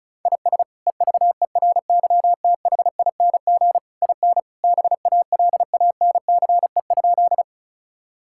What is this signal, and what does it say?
Morse code, heard as a keyed tone, about 34 wpm, 700 Hz. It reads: IS EVERYTHING IN BALANCE?
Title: Sentences from Top 1000 Words - Review Part 6 - 35wpm